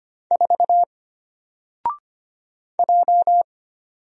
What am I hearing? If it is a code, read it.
Morse code, heard as a keyed tone, about 25 wpm, 700 Hz. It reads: 4 J
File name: Single-Letter-Number-25wpm